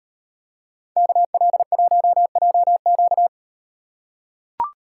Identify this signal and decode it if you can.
Morse code, heard as a keyed tone, about 35 wpm, 700 Hz. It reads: KL1JQ